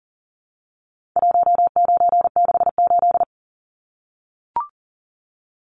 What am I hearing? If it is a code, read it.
Morse code, heard as a keyed tone, about 40 wpm, 700 Hz. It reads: 1968
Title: Numbers - 4-Digit 40wpm